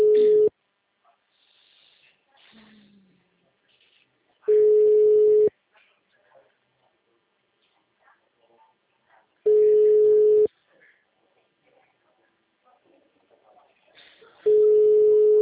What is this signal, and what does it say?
Ringback tone (Europe, 425 Hz), with speech.
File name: out- -2019- - - .20086